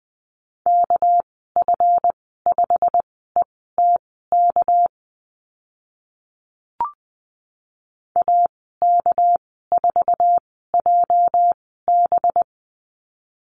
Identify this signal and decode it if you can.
Morse code, heard as a keyed tone, about 20 wpm, spaced as for 15 wpm, 700 Hz. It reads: KF5ETK AK4JB
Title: 231 Lesson 37g - Letter x - Call Signs 20wpm